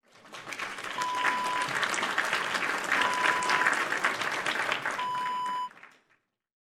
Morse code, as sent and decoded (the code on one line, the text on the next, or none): none